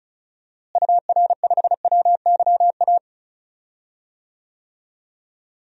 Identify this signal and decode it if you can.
Morse code, heard as a keyed tone, about 36 wpm, 700 Hz. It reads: UR5WYA
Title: DXCC-100 Call Signs - 6 Characters - Repeated 2x - 35wpm